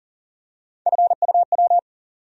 Morse code, heard as a keyed tone, about 40 wpm, 700 Hz. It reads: FUW
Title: USA Call Sign Postfixes - 3 Letters 40wpm